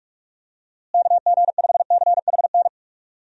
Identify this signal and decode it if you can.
Morse code, heard as a keyed tone, about 46 wpm, 700 Hz. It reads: KG5CHN
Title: USA Call Signs - 6 Characters - Repeated 3x - 45wpm